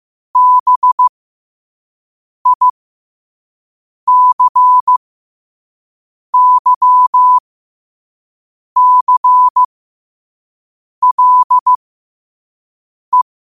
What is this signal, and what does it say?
Morse code, heard as a keyed tone, about 15 wpm, spaced as for 5 wpm, 1000 Hz. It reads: BICYCLE
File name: 5wpm